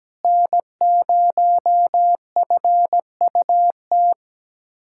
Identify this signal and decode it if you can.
Morse code, heard as a keyed tone, about 17 wpm, 700 Hz. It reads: N0FUT
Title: USA Callsign DE Callsign - 5 Characters - 17wpm